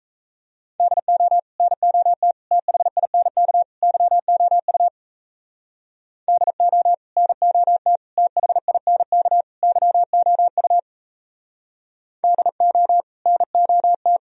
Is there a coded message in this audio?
Yes: Morse code from 0.8 to 14.2 s, heard as a keyed tone, about 36 wpm, 700 Hz. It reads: DO NOT THINK YOU DO NOT THINK YOU DO NOT